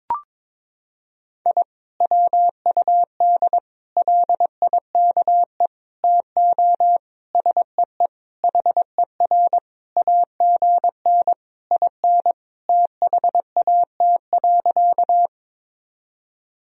Morse code, heard as a keyed tone, about 22 wpm, 700 Hz. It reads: I WUD LIKE TO SEE HER AGN IN THAT.
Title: Sentences from Top 200 Words and Abbreviations 22wpm